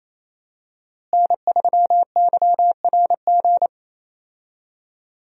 Morse code, heard as a keyed tone, about 28 wpm, 700 Hz. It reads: N3YRG
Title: USA Call Signs - 5 Characters - Repeated 3x - 28wpm